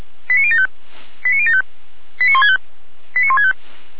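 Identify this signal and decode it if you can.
Five-tone selective call (ZVEI), heard as quick runs of five tones, tones of about 70 ms. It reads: 89075 89075 89165 89165
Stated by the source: ZVEI five-tone selective call received by SDR.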